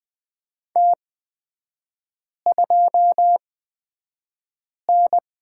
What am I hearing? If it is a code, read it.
Morse code, heard as a keyed tone, about 20 wpm, spaced as for 5 wpm, 700 Hz. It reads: T2N
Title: 236 Lesson 38e - Letter q - Three Characters 20wpm